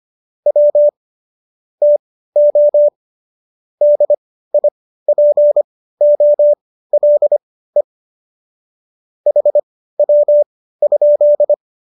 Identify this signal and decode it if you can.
Morse code, heard as a keyed tone, about 25 wpm, spaced as for 15 wpm, 590 Hz. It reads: W TO DIPOLE HW?